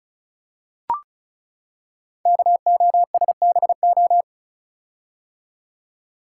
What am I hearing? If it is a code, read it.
Morse code, heard as a keyed tone, about 36 wpm, 700 Hz. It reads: KOSBO
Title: Code Groups - 5 Letters - 35wpm